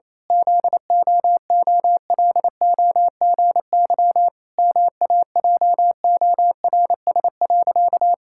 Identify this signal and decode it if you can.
Morse code, heard as a keyed tone, about 28 wpm, 700 Hz. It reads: ZOOLOGY MAJORS.
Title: Pangram Challenge v2 - 28wpm